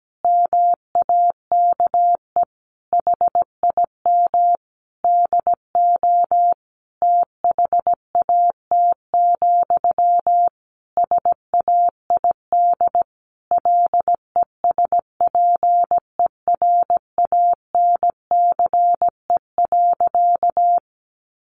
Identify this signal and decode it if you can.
Morse code, heard as a keyed tone, about 17 wpm, 700 Hz. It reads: MAKE HIM DO THAT, SAID LESPERANCE.
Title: A Sound Of Thunder 17wpm